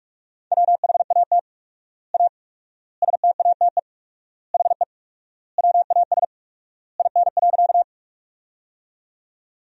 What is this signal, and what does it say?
Morse code, heard as a keyed tone, about 48 wpm, 700 Hz. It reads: WHAT A STATE HE WAS IN.
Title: 4x Word-Spacing - Sentences from Top 400 Words - 45wpm